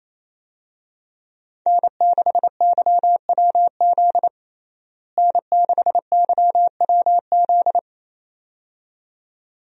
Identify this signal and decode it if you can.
Morse code, heard as a keyed tone, about 28 wpm, 700 Hz. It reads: N6YWZ N6YWZ
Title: USA Call Signs - 5 Characters - Repeated 2x - 28wpm